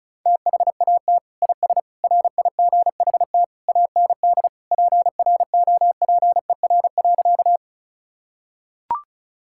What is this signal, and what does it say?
Morse code, heard as a keyed tone, about 34 wpm, 700 Hz. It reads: THAT IS RIGHT AND PROPER.
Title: Sentences from Top 1000 Words - Review Part 5 - 35wpm